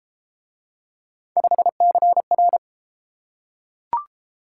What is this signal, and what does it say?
Morse code, heard as a keyed tone, about 30 wpm, 700 Hz. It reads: 5CR